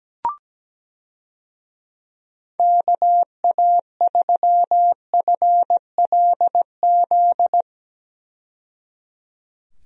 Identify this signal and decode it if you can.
Morse code, heard as a keyed tone, about 17 wpm, 700 Hz. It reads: KA3FLZ
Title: USA Call Signs - 6 Characters - 17wpm